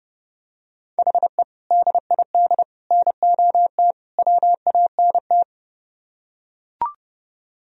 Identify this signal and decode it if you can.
Morse code, heard as a keyed tone, about 31 wpm, 700 Hz. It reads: HE DID NOT WANT